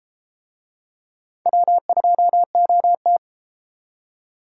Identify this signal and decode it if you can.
Morse code, heard as a keyed tone, about 33 wpm, 700 Hz. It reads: W2OT